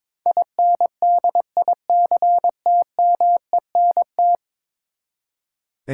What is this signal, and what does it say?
Morse code, heard as a keyed tone, about 22 wpm, 700 Hz. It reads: INDICTMENT
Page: morse-code-ninja-practice-22wpm